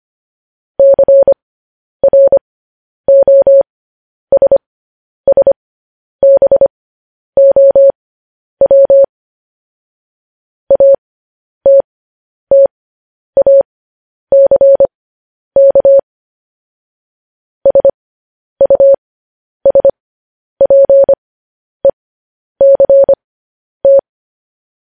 Morse code, heard as a keyed tone, about 25 wpm, spaced as for 10 wpm, 550 Hz. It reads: CROSSBOW ATTACK SUSPECT
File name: podcast / News headlines-1x-25x10